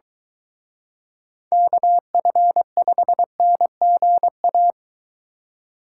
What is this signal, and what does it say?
Morse code, heard as a keyed tone, about 23 wpm, 700 Hz. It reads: KF5NGA